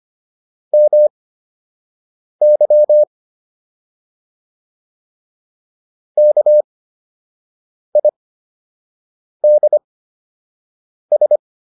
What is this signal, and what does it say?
Morse code, heard as a keyed tone, about 25 wpm, spaced as for 6 wpm, 600 Hz. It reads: MY KIDS